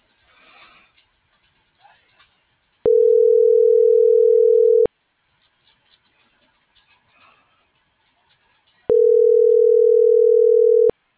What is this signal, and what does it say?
Ringback tone (North America), 440 + 480 Hz.